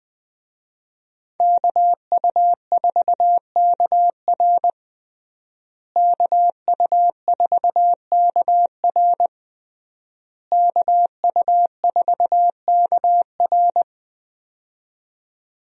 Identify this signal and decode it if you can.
Morse code, heard as a keyed tone, about 20 wpm, 700 Hz. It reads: KU4KR KU4KR KU4KR
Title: USA Call Signs - 5 Characters - Repeated 3x - 20wpm